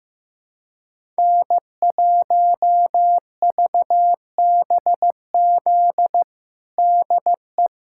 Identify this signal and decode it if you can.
Morse code, heard as a keyed tone, about 15 wpm, 700 Hz. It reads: N1VBZ DE
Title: USA Callsign DE Callsign - 5 Characters - 15wpm